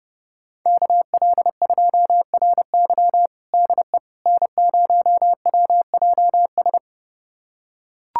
Morse code, heard as a keyed tone, about 32 wpm, 700 Hz. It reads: KL2RY DE N0WJS
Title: USA Callsign DE Callsign - 5 Characters - 30wpm